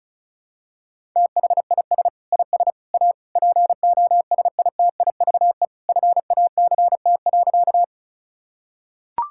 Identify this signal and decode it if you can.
Morse code, heard as a keyed tone, about 36 wpm, 700 Hz. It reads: THIS IS A POSITIVE FACT.
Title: Sentences from Top 1000 Words - Review Part 5 - 35wpm